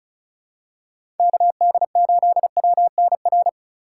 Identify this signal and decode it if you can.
Morse code, heard as a keyed tone, about 36 wpm, 700 Hz. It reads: KD8WNR
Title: USA Call Signs - 6 Characters - 3x Character Spacing - 35wpm